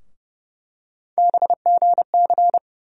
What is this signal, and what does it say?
Morse code, heard as a keyed tone, about 30 wpm, 700 Hz. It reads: BGC
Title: USA Call Sign Postfixes - 3 Letters 30wpm